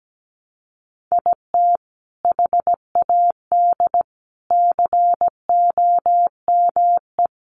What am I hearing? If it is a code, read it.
Morse code, heard as a keyed tone, about 17 wpm, 700 Hz. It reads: IT HAD COME
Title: Sentences from Top 1300 Words 17wpm